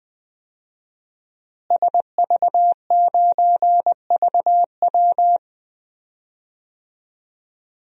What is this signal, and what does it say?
Morse code, heard as a keyed tone, about 20 wpm, 700 Hz. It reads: SV9VW
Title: DXCC-100 Call Signs - 5 Characters - Repeated 3x - 20wpm